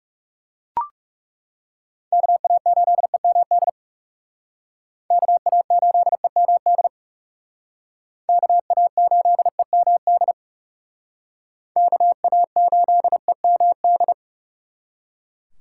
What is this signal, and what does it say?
Morse code, heard as a keyed tone, about 37 wpm, 700 Hz. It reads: KA8EMD KA8EMD KA8EMD KA8EMD
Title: Speed Racing - USA 2x3 Call Signs 30wpm